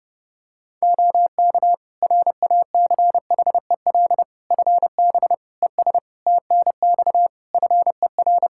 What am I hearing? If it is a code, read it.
Morse code, heard as a keyed tone, about 30 wpm, 700 Hz. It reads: OK RACHEL FB ES TNX FER
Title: K1USN SST QSO Practice 30wpm